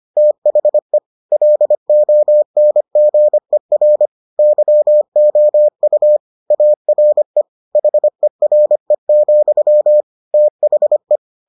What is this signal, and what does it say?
Morse code, heard as a keyed tone, about 25 wpm, 600 Hz. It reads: THE LONGER YOU ARE HERE, THE